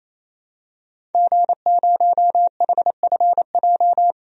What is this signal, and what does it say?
Morse code, heard as a keyed tone, about 28 wpm, 700 Hz. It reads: G0HFJ
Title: DXCC-100 Call Signs - 5 Characters - 28wpm